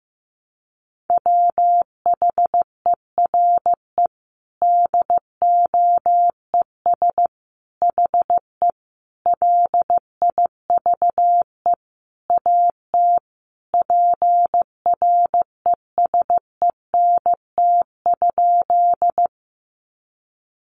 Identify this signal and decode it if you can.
Morse code, heard as a keyed tone, about 15 wpm, 700 Hz. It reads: WHERE DOES HE LIVE AT PRESENT?
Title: Sentences from Top 800 Words 15wpm